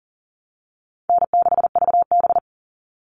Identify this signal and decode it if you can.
Morse code, heard as a keyed tone, about 40 wpm, 700 Hz. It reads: N6VB
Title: All Cut Numbers - 4-Digit 40wpm